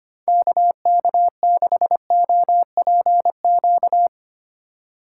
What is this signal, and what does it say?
Morse code, heard as a keyed tone, about 25 wpm, 700 Hz. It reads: KK6OPQ
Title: USA Call Signs - 2x Character Spacing - 25wpm